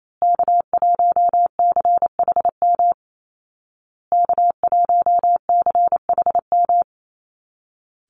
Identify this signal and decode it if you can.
Morse code, heard as a keyed tone, about 28 wpm, 700 Hz. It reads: K1CHM K1CHM
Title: Call Signs Repeated 2x - 28wpm